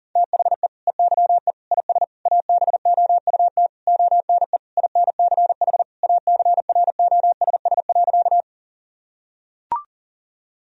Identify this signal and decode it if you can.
Morse code, heard as a keyed tone, about 40 wpm, 700 Hz. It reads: THE EYE IS ABOUT ONE INCH ACROSS.
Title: Sentences from Top 1000 Words - Review Part 3 - 40wpm